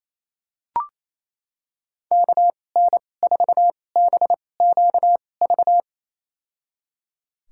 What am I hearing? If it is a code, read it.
Morse code, heard as a keyed tone, about 28 wpm, spaced as for 20 wpm, 700 Hz. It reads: KN4BQV
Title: USA Call Signs - 6 Characters - 2x Character Spacing - 28wpm